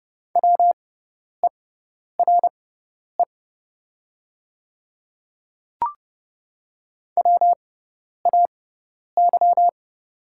Morse code, heard as a keyed tone, about 30 wpm, spaced as for 10 wpm, 700 Hz. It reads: WERE WAY
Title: 6x Character Spacing - Top 100 Words - 30wpm